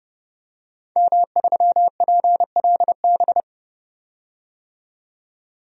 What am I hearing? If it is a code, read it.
Morse code, heard as a keyed tone, about 30 wpm, 700 Hz. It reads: M3PLB
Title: DXCC-100 Call Signs - 5 Characters - 30wpm